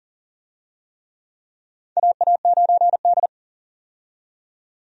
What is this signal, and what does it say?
Morse code, heard as a keyed tone, about 40 wpm, 700 Hz. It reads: AA9D
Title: USA Call Signs - 4 Characters - 40wpm